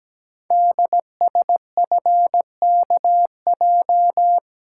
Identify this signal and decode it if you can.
Morse code, heard as a keyed tone, about 17 wpm, 700 Hz. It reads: DSFKJ
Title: Code Groups - 5 Letters - 17wpm